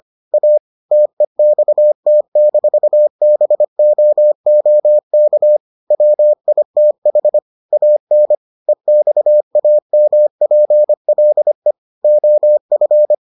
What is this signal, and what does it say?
Morse code, heard as a keyed tone, about 25 wpm, 600 Hz. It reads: A TEXT-BOOK WITH AN EXAMPLE OF